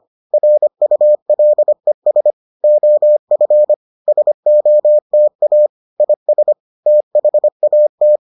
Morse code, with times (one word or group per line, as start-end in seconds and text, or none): 0.3-2.3 RULES
2.6-3.7 OF
4.1-5.7 SOTA
6.0-6.5 IS
6.9-8.2 THAT